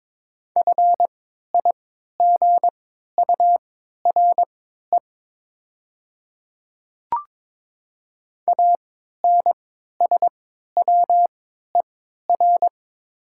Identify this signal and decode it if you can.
Morse code, heard as a keyed tone, about 22 wpm, spaced as for 13 wpm, 700 Hz. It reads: FIGURE ANSWER
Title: 3x Character Spacing - Top 500 Words - 22wpm